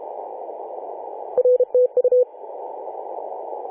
Morse code, heard as a keyed tone, about 32 wpm, 500 Hz. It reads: R TU
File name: 2013/Sep/21 2035z 3D2GC 18.072MHz CW